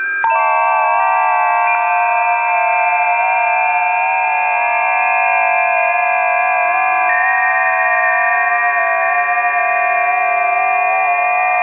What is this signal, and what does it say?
FT8.
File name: tests / 20m busy / test 25